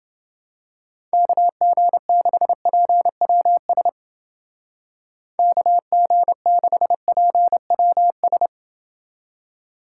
Morse code, heard as a keyed tone, about 30 wpm, 700 Hz. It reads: KG6PWS KG6PWS